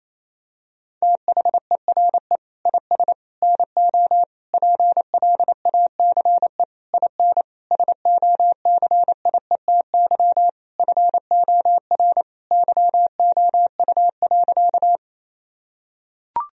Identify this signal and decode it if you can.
Morse code, heard as a keyed tone, about 28 wpm, 700 Hz. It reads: THERE IS NO PLACE IN SOCIETY FOR YOU.
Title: Sentences from Top 1000 Words - Review Part 4 - 28wpm